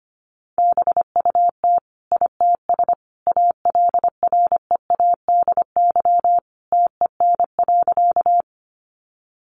Morse code, heard as a keyed tone, about 25 wpm, 700 Hz. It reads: BUT ITS ALREADY TEN.
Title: Sentences from Top 700 Words 25wpm